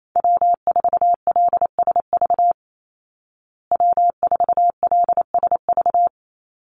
Morse code, heard as a keyed tone, about 28 wpm, 700 Hz. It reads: W4LSV W4LSV
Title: Call Signs Repeated 2x - 28wpm